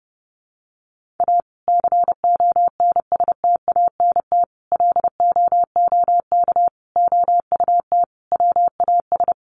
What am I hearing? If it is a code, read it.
Morse code, heard as a keyed tone, about 30 wpm, 700 Hz. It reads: A CONSTANT LOOK OUT WAS
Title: Sentences from Top 1500 Words 30wpm